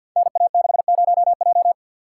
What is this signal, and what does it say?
Morse code, heard as a keyed tone, about 53 wpm, 700 Hz. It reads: NA60J
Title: Code Groups - 5 Letter:Numbers - 50wpm